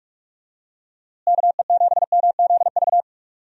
Morse code, heard as a keyed tone, about 43 wpm, 700 Hz. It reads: KE7MZV
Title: USA Call Signs - 6 Characters - Repeated 2x - 45wpm